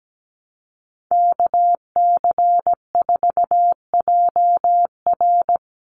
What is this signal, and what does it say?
Morse code, heard as a keyed tone, about 17 wpm, 700 Hz. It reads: KC4JR
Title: Common QSO Phrases Encore 17wpm